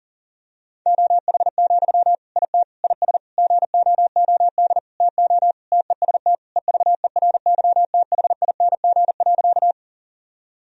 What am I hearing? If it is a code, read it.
Morse code, heard as a keyed tone, about 40 wpm, 700 Hz. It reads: OH, IT IS GOOD TO TEST EVERYTHING.
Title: Sentences from Top 1000 Words - Review Part 3 - 40wpm